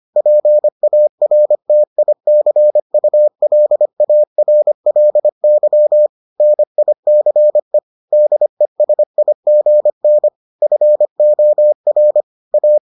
Morse code, heard as a keyed tone, about 25 wpm, 600 Hz. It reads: PARTICULARLY NICE DESIGN FOR A